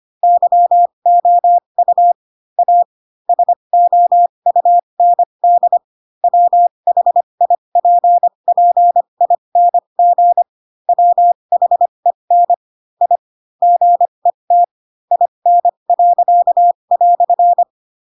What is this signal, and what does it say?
Morse code, heard as a keyed tone, about 25 wpm, spaced as for 22 wpm, 700 Hz. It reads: YOU A SOUND WHIPPING WHEN I GET IN."
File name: Pinocchio / Pinocchio ch7 18 25 ik1hge